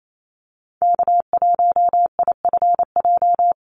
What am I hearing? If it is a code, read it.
Morse code, heard as a keyed tone, about 28 wpm, 700 Hz. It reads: K1IFJ
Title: Call Signs Repeated 3x - 28wpm